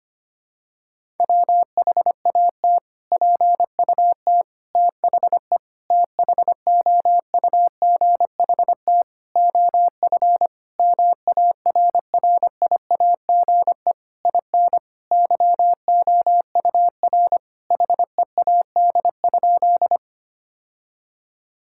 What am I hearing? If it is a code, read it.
Morse code, heard as a keyed tone, about 25 wpm, 700 Hz. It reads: WHAT PUT THE THOUGHT OF MARRIAGE IN YOUR HEAD?
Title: Sentences from Top 1000 Words - Review Part 5 - 25wpm